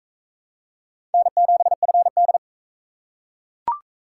Morse code, heard as a keyed tone, about 42 wpm, 700 Hz. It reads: N7FD